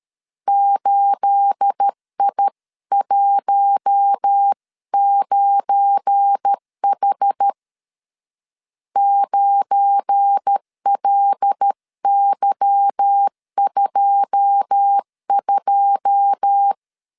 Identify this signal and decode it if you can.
Morse code, heard as a keyed tone, about 13 wpm, 800 Hz. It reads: 8I19H 9LY22
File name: CW / DET / 1 9 1500C S C 12wpm 3811